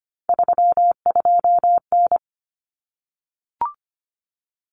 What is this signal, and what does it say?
Morse code, heard as a keyed tone, about 25 wpm, 700 Hz. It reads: 32N
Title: Common Cut Numbers - 3-Digit 25wpm